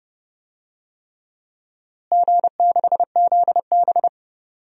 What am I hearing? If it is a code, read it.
Morse code, heard as a keyed tone, about 30 wpm, 700 Hz. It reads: G6ZB
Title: DXCC-100 Call Signs - 4 Characters - Repeated 3x - 30wpm